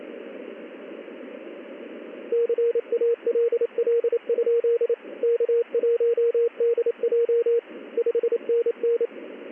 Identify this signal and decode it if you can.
Morse code, heard as a keyed tone, about 28 wpm, 470 Hz, with radio noise. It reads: CALL? K1DJ 5NN
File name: contests / ss-ssb-k5tr- -1700